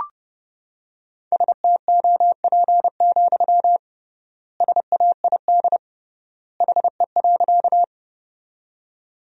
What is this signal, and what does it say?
Morse code, heard as a keyed tone, about 30 wpm, 700 Hz. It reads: STOP, SAID HE.
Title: 3x Word-Spacing - Sentences from Top 300 Words - 30wpm